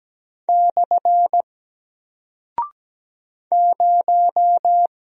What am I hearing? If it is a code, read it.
Morse code, heard as a keyed tone, about 17 wpm, 700 Hz. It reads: / 0
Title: Single Character (Letter:Number:Punctuation:Prosign) Alt 17wpm